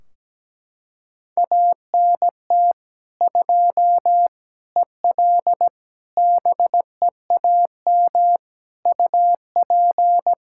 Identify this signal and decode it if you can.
Morse code, heard as a keyed tone, about 17 wpm, 700 Hz. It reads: ANT 2 EL BEAM UP
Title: K1USN SST QSO Practice 17wpm